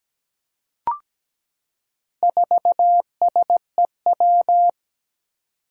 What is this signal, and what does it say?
Morse code, heard as a keyed tone, about 17 wpm, 700 Hz. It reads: 4SEW
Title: Four Challenging Letter-Number Combinations 17wpm